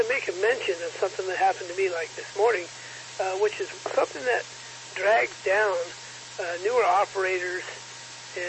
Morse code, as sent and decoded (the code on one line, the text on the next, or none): none